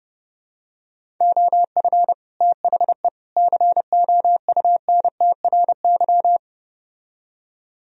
Morse code, heard as a keyed tone, about 30 wpm, 700 Hz. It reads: OF THE COUNTRY